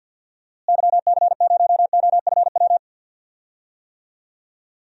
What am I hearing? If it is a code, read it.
Morse code, heard as a keyed tone, about 52 wpm, 700 Hz. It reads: YC0OFW
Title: DXCC-100 Call Signs - 6 Characters - Repeated 3x - 50wpm